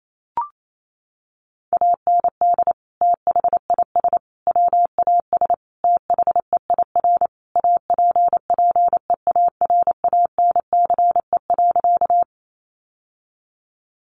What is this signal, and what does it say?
Morse code, heard as a keyed tone, about 28 wpm, 700 Hz. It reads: AND THIS WAS THEIR APPEARANCE.
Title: Sentences from Top 1100 Words 28wpm